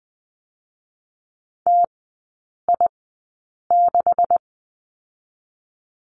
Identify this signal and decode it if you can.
Morse code, heard as a keyed tone, about 20 wpm, spaced as for 8 wpm, 700 Hz. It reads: TI6
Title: 218 Lesson 35e - Number 0 - Three Characters 20wpm